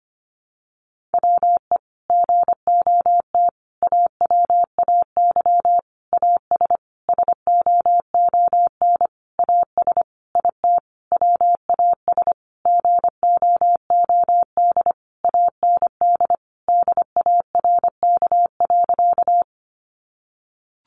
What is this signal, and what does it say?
Morse code, heard as a keyed tone, about 25 wpm, 700 Hz. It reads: WE GOT AWAY AS SOON AS IT WAS GOOD AND DARK.
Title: Sentences from Top 600 Words 25wpm